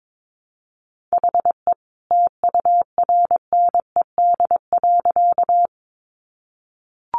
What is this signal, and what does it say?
Morse code, heard as a keyed tone, about 22 wpm, 700 Hz. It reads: HE TURNED.
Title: A Sound Of Thunder 22wpm